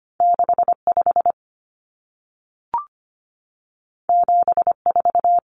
Morse code, heard as a keyed tone, about 25 wpm, 700 Hz. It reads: 65 74
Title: Numbers - 2-Digit 25wpm